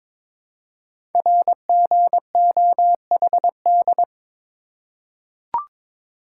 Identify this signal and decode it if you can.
Morse code, heard as a keyed tone, about 22 wpm, 700 Hz. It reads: RGOHD
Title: Code Groups - 5 Letters - 22wpm